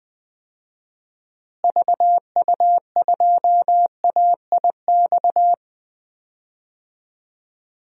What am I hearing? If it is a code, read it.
Morse code, heard as a keyed tone, about 20 wpm, 700 Hz. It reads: VU2AIX
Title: DXCC-100 Call Signs - 6 Characters - Repeated 2x - 20wpm